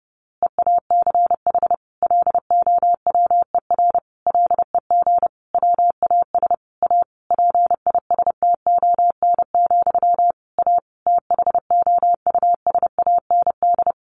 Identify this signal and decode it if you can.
Morse code, heard as a keyed tone, about 30 wpm, 700 Hz. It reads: EACH LOWER LEG WAS A PISTON, A THOUSAND
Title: A Sound Of Thunder 30wpm